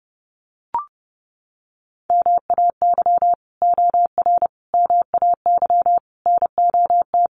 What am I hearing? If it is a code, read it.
Morse code, heard as a keyed tone, about 30 wpm, 700 Hz. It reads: MAY OR MAY NOT
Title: Sets of 5 Words 30wpm